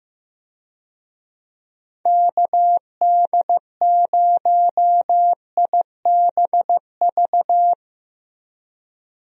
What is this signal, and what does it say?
Morse code, heard as a keyed tone, about 15 wpm, 700 Hz. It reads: KD0IBV
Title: USA Call Signs - 6 Characters - Repeated 2x - 15wpm